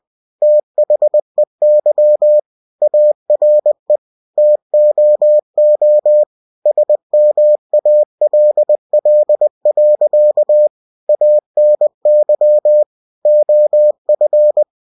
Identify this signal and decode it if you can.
Morse code, heard as a keyed tone, about 20 wpm, 600 Hz. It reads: THEY ARE TOO SMALL. ANY OF
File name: cw-books / winnie-the-pooh / winnie-20wpm-page-0107